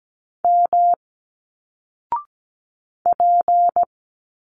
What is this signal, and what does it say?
Morse code, heard as a keyed tone, about 17 wpm, 700 Hz. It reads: M P
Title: Single Letters 17wpm